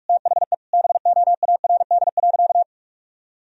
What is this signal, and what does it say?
Morse code, heard as a keyed tone, about 46 wpm, 700 Hz. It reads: THE BOARD.
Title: Sentences from Top 1000 Words - Review Part 4 - 45wpm